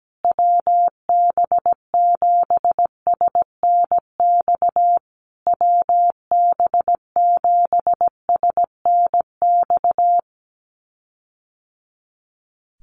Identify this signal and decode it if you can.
Morse code, heard as a keyed tone, about 17 wpm, 700 Hz. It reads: WB7SNX WB7SNX
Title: USA Call Signs Repeated 3x 17wpm